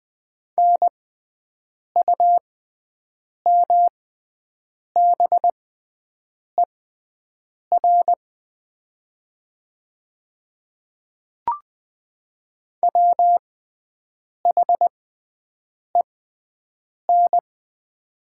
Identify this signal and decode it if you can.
Morse code, heard as a keyed tone, about 20 wpm, spaced as for 7 wpm, 700 Hz. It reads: NUMBER WHEN
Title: 6x Character Spacing - Top 100 Words - 20wpm